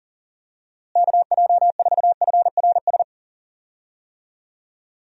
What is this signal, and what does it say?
Morse code, heard as a keyed tone, about 40 wpm, 700 Hz. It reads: KJ4FRS
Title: USA Call Signs - 6 Characters - Repeated 3x - 40wpm